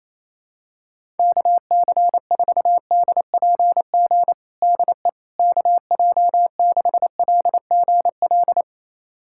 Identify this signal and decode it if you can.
Morse code, heard as a keyed tone, about 28 wpm, 700 Hz. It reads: KC4DPG DE KJ6LGL
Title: USA Callsign DE Callsign - 6 Characters - 28wpm